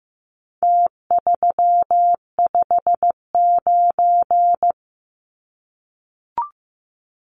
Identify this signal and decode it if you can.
Morse code, heard as a keyed tone, about 15 wpm, 700 Hz. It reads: T359
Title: Numbers - Contest Style - 4-Digit 15wpm